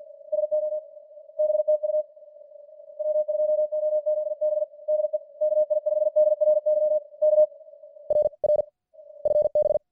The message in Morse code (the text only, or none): SO BTU W2OZB DE KI5DRY K R R FB